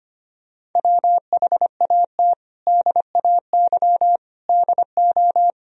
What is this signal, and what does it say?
Morse code, heard as a keyed tone, about 25 wpm, 700 Hz. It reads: WHAT DAY DO
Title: Sentences from Top 1000 Words - Review Part 3 - 25wpm